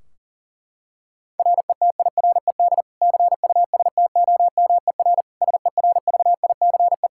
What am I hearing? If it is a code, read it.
Morse code, heard as a keyed tone, about 40 wpm, 700 Hz. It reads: RETIRED CUSTOMER SERVICE
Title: Practice QSO - Exchange 3 - More Info - 40wpm